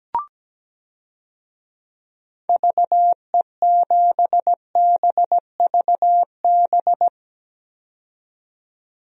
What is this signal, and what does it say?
Morse code, heard as a keyed tone, about 17 wpm, 700 Hz. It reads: VE7BVB
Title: DXCC-100 Call Signs - 6 Characters - 17wpm